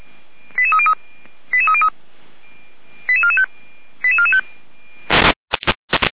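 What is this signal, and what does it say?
Five-tone selective call (ZVEI), heard as quick runs of five tones, tones of about 70 ms. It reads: 88303 88303 88445 88445
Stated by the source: ZVEI five-tone selective call, code 08826, received on an SDR.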